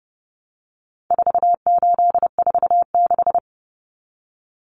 Morse code, heard as a keyed tone, about 30 wpm, 700 Hz. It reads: 4846
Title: Common Cut Numbers - 4-Digit 30wpm